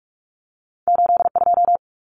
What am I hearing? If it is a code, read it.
Morse code, heard as a keyed tone, about 46 wpm, 700 Hz. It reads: 82
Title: Numbers - 2-Digit 45wpm